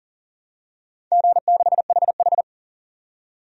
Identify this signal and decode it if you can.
Morse code, heard as a keyed tone, about 41 wpm, 700 Hz. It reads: G6HH
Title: DXCC-100 Call Signs - 4 Characters - 40wpm